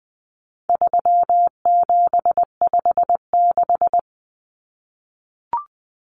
Morse code, heard as a keyed tone, about 20 wpm, 700 Hz. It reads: 3756
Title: Common Cut Numbers - 4-Digit 20wpm